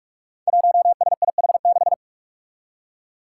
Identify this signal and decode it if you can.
Morse code, heard as a keyed tone, about 44 wpm, 700 Hz. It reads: 1SIH6
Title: Five Challenging Letter-Number Combinations 45wpm